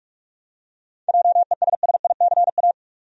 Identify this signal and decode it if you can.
Morse code, heard as a keyed tone, about 46 wpm, 700 Hz. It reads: JESSICA
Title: USA Common Names - 45wpm